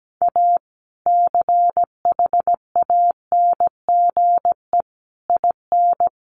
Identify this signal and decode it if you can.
Morse code, heard as a keyed tone, about 17 wpm, 700 Hz. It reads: A CHANGE IN
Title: Sentences from Top 1500 Words 17wpm